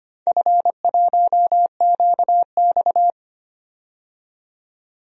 Morse code, heard as a keyed tone, about 25 wpm, 690 Hz. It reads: F1QX
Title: DXCC-100 Call Signs - 4 Characters - 25wpm